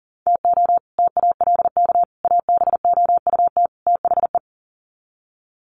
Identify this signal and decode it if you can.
Morse code, heard as a keyed tone, about 40 wpm, 700 Hz. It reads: TO TALK ABOUT THE
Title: Sets of 4 Words 40wpm